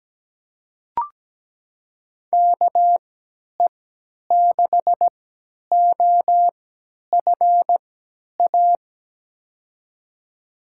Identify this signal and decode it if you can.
Morse code, heard as a keyed tone, about 17 wpm, spaced as for 10 wpm, 700 Hz. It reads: KE6OFA
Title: USA Call Signs - 6 Characters - 3x Character Spacing - 17wpm